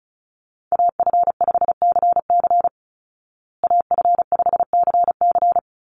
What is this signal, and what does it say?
Morse code, heard as a keyed tone, about 34 wpm, 700 Hz. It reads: AF5CC AF5CC
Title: Call Signs Repeated 3x - 35wpm